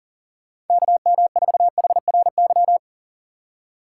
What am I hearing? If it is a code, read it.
Morse code, heard as a keyed tone, about 37 wpm, 700 Hz. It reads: KM4HRY
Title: USA Call Signs - 6 Characters - 6x Character Spacing - 40wpm